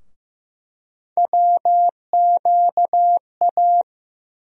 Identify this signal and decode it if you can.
Morse code, heard as a keyed tone, about 15 wpm, 700 Hz. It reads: WQA
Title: USA Call Sign Postfixes - 3 Letters 15wpm